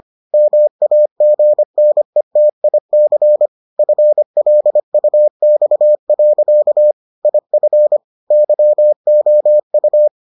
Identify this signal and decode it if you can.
Morse code, heard as a keyed tone, about 25 wpm, 600 Hz. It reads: MAGNETIC FLUX. IF YOU